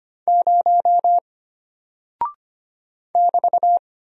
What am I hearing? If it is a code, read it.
Morse code, heard as a keyed tone, about 25 wpm, 700 Hz. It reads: 0 <BT>
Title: Single Character (Letter:Number:Punctuation:Prosign) Alt 25wpm